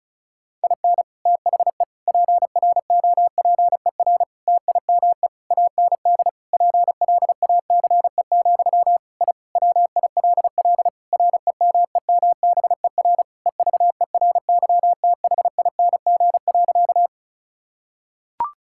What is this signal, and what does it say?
Morse code, heard as a keyed tone, about 34 wpm, 700 Hz. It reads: IN THE PROPER TIME AND PLACE, I WILL REMEMBER EVERYTHING.
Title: Sentences from Top 1000 Words - Review Part 5 - 35wpm